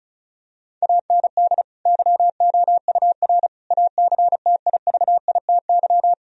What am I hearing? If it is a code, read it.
Morse code, heard as a keyed tone, about 36 wpm, 700 Hz. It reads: AND YOUR ACTIVITY
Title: Sentences from Top 1000 Words - Review Part 4 - 35wpm